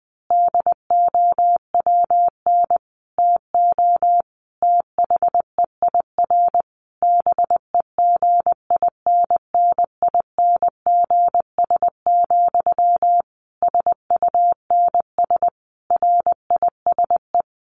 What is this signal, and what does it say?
Morse code, heard as a keyed tone, about 20 wpm, 700 Hz. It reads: DOWN TO THEIR BEGINNINGS, SUNS RISE
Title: A Sound Of Thunder 20wpm